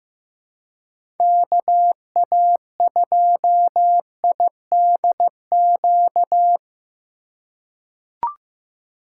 Morse code, heard as a keyed tone, about 15 wpm, 700 Hz. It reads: KA2IDQ
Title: USA Call Signs - 6 Characters - 4x Character Spacing - 15wpm